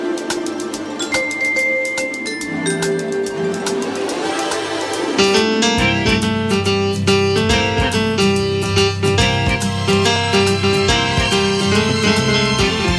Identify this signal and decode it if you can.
Dial-up modem handshake.
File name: wrct-log hi- 19